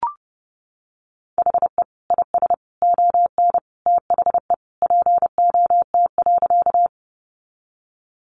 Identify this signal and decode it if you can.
Morse code, heard as a keyed tone, about 30 wpm, 700 Hz. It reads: HE IS ON THE POT.
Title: Sentences from Top 1000 Words 30wpm